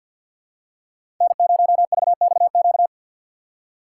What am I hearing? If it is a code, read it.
Morse code, heard as a keyed tone, about 52 wpm, 690 Hz. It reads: N0VXQ
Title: USA Call Signs - 5 Characters - 4x Character Spacing - 50wpm